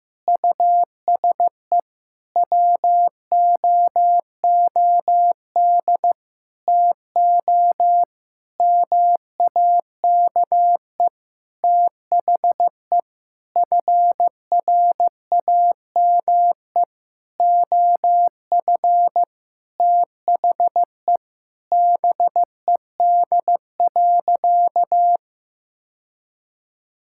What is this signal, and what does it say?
Morse code, heard as a keyed tone, about 15 wpm, 700 Hz. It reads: USE WOOD TO MAKE THE FRAME OF THE BED.
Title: Sentences from Top 1000 Words - Review Part 6 - 15wpm